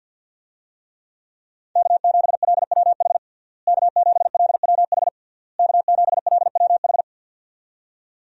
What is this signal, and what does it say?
Morse code, heard as a keyed tone, about 51 wpm, 700 Hz. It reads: K7LWH K7LWH K7LWH
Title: USA Call Signs - 5 Characters - Repeated 3x - 50wpm